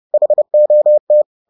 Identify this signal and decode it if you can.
Morse code, heard as a keyed tone, about 30 wpm, 600 Hz. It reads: HOT
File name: 30wpm-4